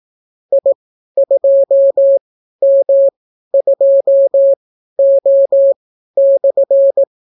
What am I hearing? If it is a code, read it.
Morse code, heard as a keyed tone, about 18 wpm, spaced as for 12 wpm, 550 Hz. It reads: I2M2O/